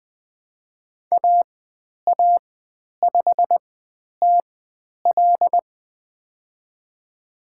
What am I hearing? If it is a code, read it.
Morse code, heard as a keyed tone, about 20 wpm, spaced as for 10 wpm, 700 Hz. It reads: AA5TL
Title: 095 Lesson 16g - Letter c - Call Signs 20wpm